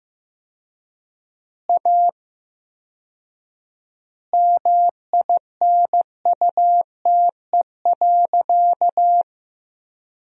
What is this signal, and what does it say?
Morse code, heard as a keyed tone, about 15 wpm, 700 Hz. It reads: A MINUTE.
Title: 4x Word-Spacing - Sentences from Top 500 Words - 15wpm